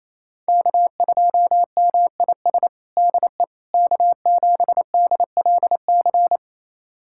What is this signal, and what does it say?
Morse code, heard as a keyed tone, about 27 wpm, 700 Hz. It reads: K2MIS DE K7DLC
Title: USA Callsign DE Callsign - 5 Characters - 28wpm